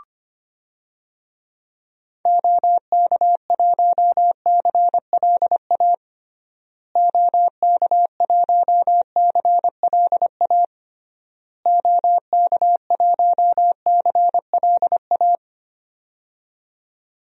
Morse code, heard as a keyed tone, about 25 wpm, 700 Hz. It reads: OK1CLA OK1CLA OK1CLA
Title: DXCC-100 Call Signs - 6 Characters - Repeated 3x - 25wpm